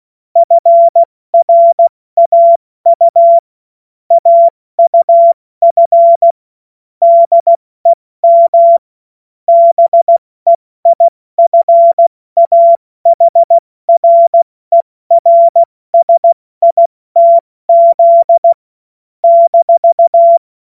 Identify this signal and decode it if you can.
Morse code, heard as a keyed tone, about 16 wpm, spaced as for 14 wpm, 680 Hz. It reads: FRAU AUF DEM BEIFAHRERSITZ -
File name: Morsekurs / Witze / Witz 26. Oktober 2021, 70 F80